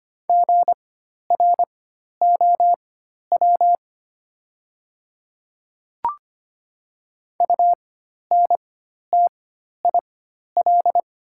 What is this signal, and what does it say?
Morse code, heard as a keyed tone, about 25 wpm, spaced as for 12 wpm, 700 Hz. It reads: GROW UNTIL
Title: 4x Character Spacing - Top 300 Words - 25wpm